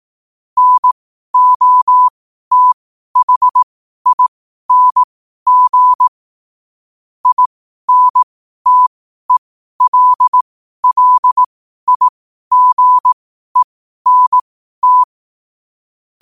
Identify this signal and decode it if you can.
Morse code, heard as a keyed tone, about 18 wpm, spaced as for 12 wpm, 1000 Hz. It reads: NOTHING INTELLIGENT
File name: qotdfiles / 12wpm